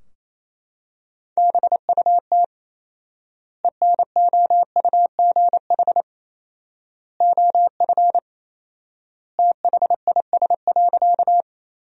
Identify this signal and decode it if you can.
Morse code, heard as a keyed tone, about 28 wpm, 700 Hz. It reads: BUT ENOUGH OF THIS.
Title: 4x Word-Spacing - Sentences from Top 400 Words - 28wpm